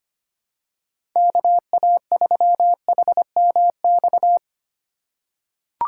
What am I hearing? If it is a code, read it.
Morse code, heard as a keyed tone, about 25 wpm, 700 Hz. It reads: KA3HMX
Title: USA Call Signs - 6 Characters - 4x Character Spacing - 25wpm